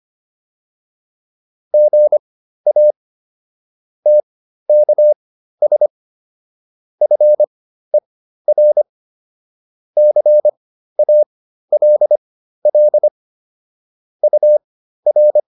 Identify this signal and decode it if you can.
Morse code, heard as a keyed tone, about 25 wpm, spaced as for 13 wpm, 600 Hz. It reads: GA TKS FER CALL UR